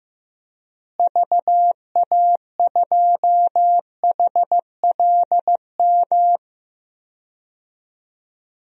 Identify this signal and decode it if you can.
Morse code, heard as a keyed tone, about 15 wpm, 700 Hz. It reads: VA2HLM
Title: DXCC-100 Call Signs - 6 Characters - Repeated 3x - 15wpm